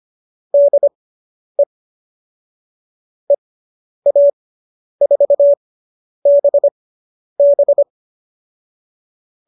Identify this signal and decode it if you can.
Morse code, heard as a keyed tone, about 25 wpm, spaced as for 10 wpm, 590 Hz. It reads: DE EA4BB